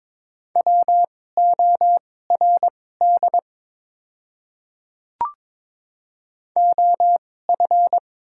Morse code, heard as a keyed tone, about 22 wpm, spaced as for 16 wpm, 700 Hz. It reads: WORD OF
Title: 2x Character Spacing - Top 100 Words - 22wpm